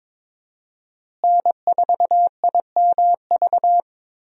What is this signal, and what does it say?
Morse code, heard as a keyed tone, about 22 wpm, 700 Hz. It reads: N4IMV
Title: USA Call Signs - 5 Characters - Repeated 2x - 22wpm